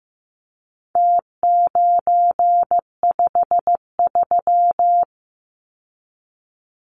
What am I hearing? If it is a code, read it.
Morse code, heard as a keyed tone, about 15 wpm, 700 Hz. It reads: T953
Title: Numbers - Contest Style - 4-Digit 15wpm